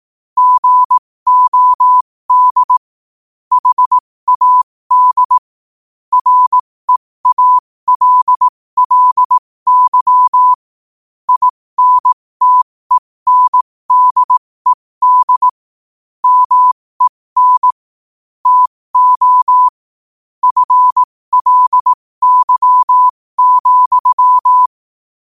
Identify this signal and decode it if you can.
Morse code, heard as a keyed tone, about 18 wpm, spaced as for 15 wpm, 1000 Hz. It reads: GOD HAD REALLY INTENDED MEN TO FLY,
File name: qotdfiles / 15wpm